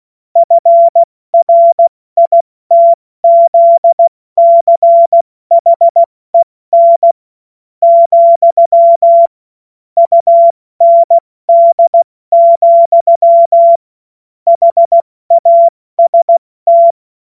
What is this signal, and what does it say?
Morse code, heard as a keyed tone, about 16 wpm, spaced as for 14 wpm, 680 Hz. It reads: FRITZCHEN, UND, HAST
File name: Morsekurs / Witze / Witz 24. August 2021, 70 F80